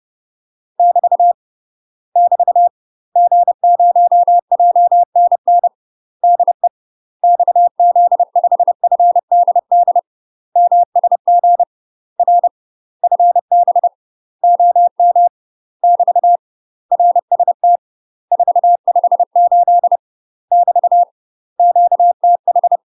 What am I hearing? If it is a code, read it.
Morse code, heard as a keyed tone, about 30 wpm, 700 Hz. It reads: <BT> <BT> G0JNN DE XZ5FDD MSG R FB OM <BT> RST 458 <BT> QTH
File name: UK MASTER QSOs TRAINING / 30WPM / UK MASTER QSOs 0008